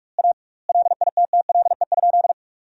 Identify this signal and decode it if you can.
Morse code, heard as a keyed tone, about 48 wpm, 700 Hz. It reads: A LITTLE?
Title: 2x Word-Spacing - Sentences from Top 500 Words - 45wpm